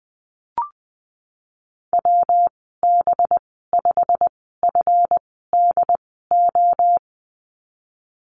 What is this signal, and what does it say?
Morse code, heard as a keyed tone, about 19 wpm, spaced as for 14 wpm, 700 Hz. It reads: WB5FDO
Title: 238 Lesson 38g - Letter q - Call Signs 20wpm